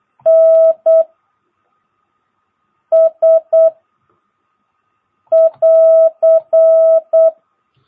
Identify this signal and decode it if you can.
Morse code, heard as a keyed tone, about 8 wpm, 650 Hz. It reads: N S <AR>
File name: jh8cbh / 2022cw